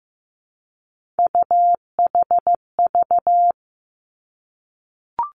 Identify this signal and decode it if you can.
Morse code, heard as a keyed tone, about 15 wpm, 700 Hz. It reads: UHV
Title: Three Challenging Letter-Number Combinations 15wpm